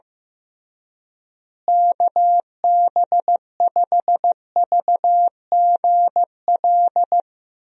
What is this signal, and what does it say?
Morse code, heard as a keyed tone, about 15 wpm, 700 Hz. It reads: KB5VGL
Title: USA Call Signs - 6 Characters - Repeated 3x - 15wpm